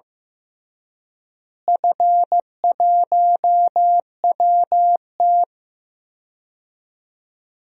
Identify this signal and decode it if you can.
Morse code, heard as a keyed tone, about 15 wpm, 700 Hz. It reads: F1WT
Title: DXCC-100 Call Signs - 4 Characters - Repeated 3x - 15wpm